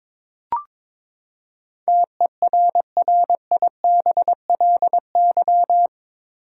Morse code, heard as a keyed tone, about 22 wpm, 700 Hz. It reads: TERRIBLY
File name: practice / 22wpm / Eight Letter Words - 22wpm